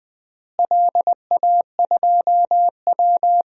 Morse code, heard as a keyed tone, about 20 wpm, 690 Hz. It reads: LA2W
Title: DXCC-100 Call Signs - 4 Characters - Repeated 3x - 20wpm